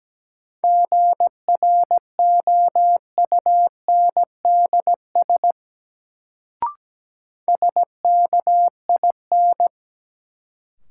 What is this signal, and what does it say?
Morse code, heard as a keyed tone, about 17 wpm, 700 Hz. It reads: GROUNDS SKIN
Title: Sentences from Top 1300 Words - New Word Drill Alt - 17wpm